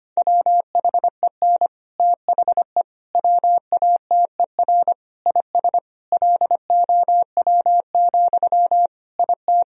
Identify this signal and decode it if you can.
Morse code, heard as a keyed tone, about 25 wpm, 700 Hz. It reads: WHEN THE WATER IS LOW, IT
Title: Sentences from Top 1000 Words - Review Part 1 - 25wpm